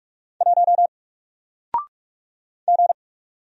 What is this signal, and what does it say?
Morse code, heard as a keyed tone, about 46 wpm, 700 Hz. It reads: 1 G
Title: Single Character (Letter:Number:Punctuation:Prosign) 45wpm